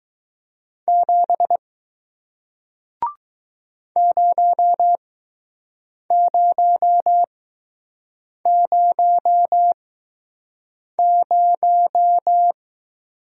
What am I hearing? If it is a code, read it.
Morse code, heard as a keyed tone, about 20 wpm, spaced as for 6 wpm, 700 Hz. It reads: 7 0000